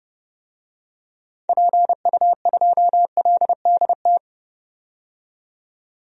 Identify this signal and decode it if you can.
Morse code, heard as a keyed tone, about 30 wpm, 700 Hz. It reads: PU2LDT
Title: DXCC-100 Call Signs - 6 Characters - 30wpm